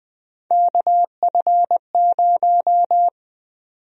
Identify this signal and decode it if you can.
Morse code, heard as a keyed tone, about 20 wpm, 700 Hz. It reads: KF0
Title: USA Call Sign Prefixes - 20wpm